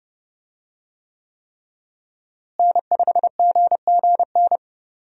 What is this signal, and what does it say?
Morse code, heard as a keyed tone, about 32 wpm, 700 Hz. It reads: N5GGN
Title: USA Call Signs - 5 Characters - Repeated 3x - 30wpm